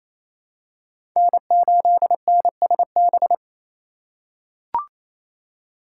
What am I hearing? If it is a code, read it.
Morse code, heard as a keyed tone, about 28 wpm, 700 Hz. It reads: N8NSB
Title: USA Call Signs - 5 Characters - 4x Character Spacing - 28wpm